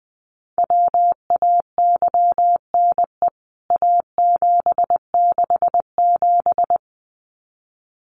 Morse code, heard as a keyed tone, about 20 wpm, 700 Hz. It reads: WAYNE A767
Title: CWT Exchange 20wpm